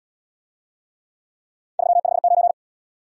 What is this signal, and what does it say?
Morse code, heard as a keyed tone, about 66 wpm, 700 Hz. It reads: 359